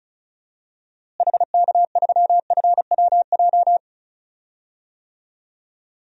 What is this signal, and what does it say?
Morse code, heard as a keyed tone, about 34 wpm, 700 Hz. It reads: HK3FWJ
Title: DXCC-100 Call Signs - 6 Characters - Repeated 2x - 35wpm